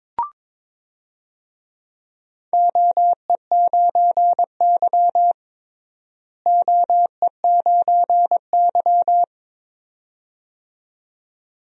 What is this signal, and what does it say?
Morse code, heard as a keyed tone, about 22 wpm, 700 Hz. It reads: OE9Y OE9Y
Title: DXCC-100 Call Signs - 4 Characters - Repeated 2x - 22wpm